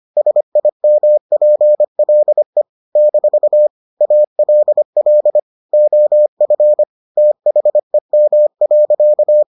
Morse code, heard as a keyed tone, about 25 wpm, 600 Hz. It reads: SIMPLE - ALL OF THEM.